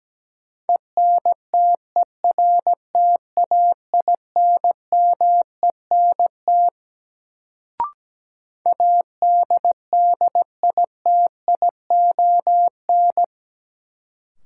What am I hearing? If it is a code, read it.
Morse code, heard as a keyed tone, about 17 wpm, 700 Hz. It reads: ENTERTAINMENT ADDITION
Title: Sentences from Top 1000 Words - Review Part 6 - New Word Drill - 17wpm